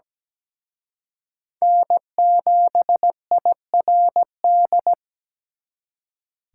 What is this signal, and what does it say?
Morse code, heard as a keyed tone, about 20 wpm, 700 Hz. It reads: N7IRD
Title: Speed Racing - USA 1x3 Call Signs 17wpm